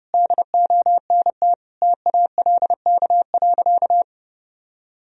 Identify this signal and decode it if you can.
Morse code, heard as a keyed tone, about 30 wpm, 700 Hz. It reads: DONT TALK.
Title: Sentences from Top 1000 Words - Review Part 2 - 30wpm